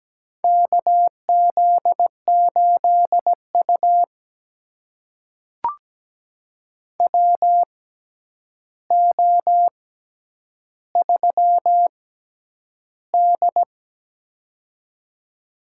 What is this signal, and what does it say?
Morse code, heard as a keyed tone, about 17 wpm, spaced as for 9 wpm, 700 Hz. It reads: KZ8U WO3D
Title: USA Call Signs - 4 Characters - 6x Character Spacing - 17wpm